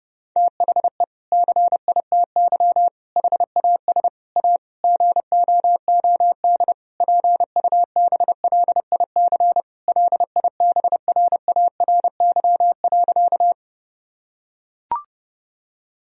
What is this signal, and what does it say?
Morse code, heard as a keyed tone, about 32 wpm, 700 Hz. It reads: THE CITY HAS A GOOD PUBLIC LIBRARY.
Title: Sentences from Top 1000 Words - Review Part 4 - 30wpm